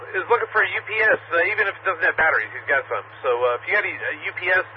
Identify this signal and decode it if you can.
Emergency Alert System (SAME) tones, heard as harsh buzzing bursts, with speech. It is heard from 0.5 to 4.0 s.